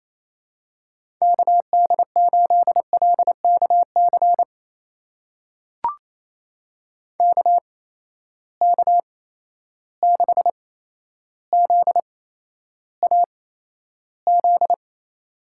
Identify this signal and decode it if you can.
Morse code, heard as a keyed tone, about 29 wpm, spaced as for 12 wpm, 700 Hz. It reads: KD8LKC KK6ZAZ
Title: USA Call Signs - 6 Characters - 8x Character Spacing - 28wpm